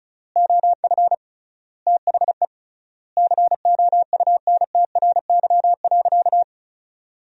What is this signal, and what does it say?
Morse code, heard as a keyed tone, about 36 wpm, 700 Hz. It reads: OF THE COUNTRY.
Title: 3x Word-Spacing - Sentences from Top 300 Words - 35wpm